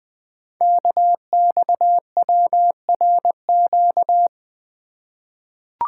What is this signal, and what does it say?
Morse code, heard as a keyed tone, about 20 wpm, 700 Hz. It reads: KXWRQ
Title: Code Groups - 5 Letters - 20wpm